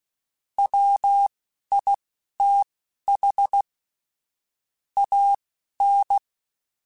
Morse code, heard as a keyed tone, about 16 wpm, spaced as for 11 wpm, 800 Hz. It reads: WITH AN
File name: Technical Subjects / Technical Subjects-mp3s / Sw Pwr Sup @10